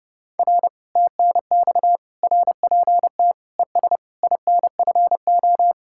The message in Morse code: R TNX RPT ES INFO

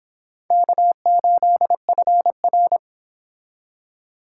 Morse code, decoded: K8FR

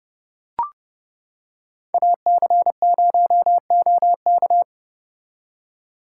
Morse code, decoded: AC0OK

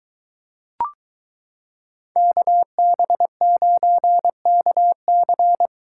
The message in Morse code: KB9KC

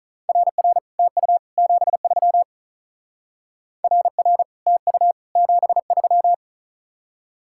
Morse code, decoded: RR TU 73 RR TU 73

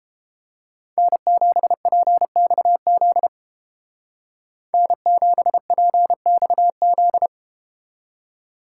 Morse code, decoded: N7PXZ N7PXZ

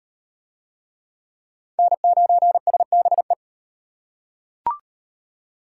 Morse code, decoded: N9SBE

